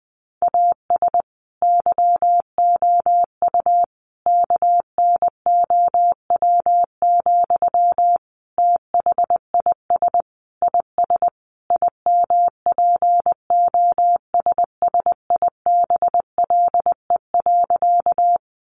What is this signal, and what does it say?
Morse code, heard as a keyed tone, about 19 wpm, 700 Hz. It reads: AS YOU KNOW, THIS IS IMPOSSIBLE.
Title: Sentences from Top 1100 Words 20wpm